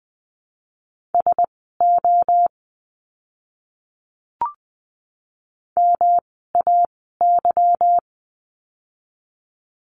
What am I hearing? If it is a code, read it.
Morse code, heard as a keyed tone, about 20 wpm, spaced as for 15 wpm, 700 Hz. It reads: SO MAY
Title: 230 Lesson 37f - Letter x - Words 20wpm